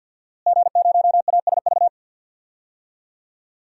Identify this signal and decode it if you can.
Morse code, heard as a keyed tone, about 52 wpm, 700 Hz. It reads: G0ASV
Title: DXCC-100 Call Signs - 5 Characters - Repeated 3x - 50wpm